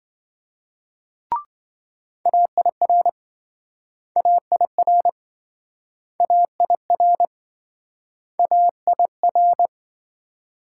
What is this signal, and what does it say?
Morse code, heard as a keyed tone, about 27 wpm, 700 Hz. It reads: AIR AIR AIR AIR